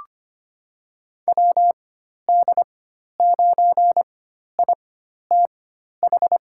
Morse code, heard as a keyed tone, about 25 wpm, spaced as for 12 wpm, 700 Hz. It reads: WD9ITH
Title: USA Call Signs - 4x Character Spacing - 25wpm